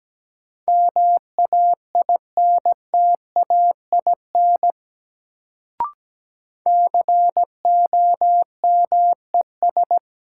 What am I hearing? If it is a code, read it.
Morse code, heard as a keyed tone, about 17 wpm, 700 Hz. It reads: MAINTAIN COMES
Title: Sentences from Top 1000 Words - Review Part 5 - New Word Drill - 17wpm